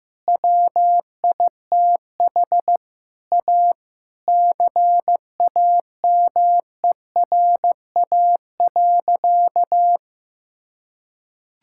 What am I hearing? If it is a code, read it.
Morse code, heard as a keyed tone, about 15 wpm, 700 Hz. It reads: WITH A CAMERA.
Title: Sentences from Top 1000 Words - Review Part 5 - 15wpm